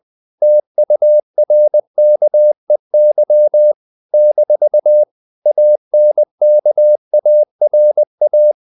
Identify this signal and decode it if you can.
Morse code, heard as a keyed tone, about 20 wpm, 600 Hz. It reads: TURKEY - ANKARA
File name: CW / Practice / NationalCapitals20